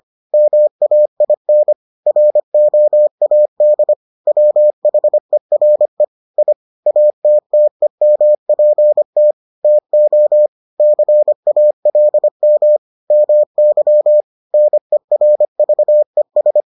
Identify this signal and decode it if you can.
Morse code, heard as a keyed tone, about 25 wpm, 600 Hz. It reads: MAIN ROAD WHERE I ATTEMPT TO CALM MY NERVES